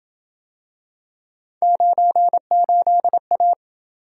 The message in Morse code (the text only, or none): 98A